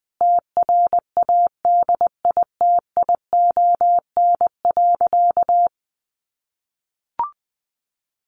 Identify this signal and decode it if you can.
Morse code, heard as a keyed tone, about 20 wpm, 700 Hz. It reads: TRADITION.
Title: Sentences from Top 1500 Words 20wpm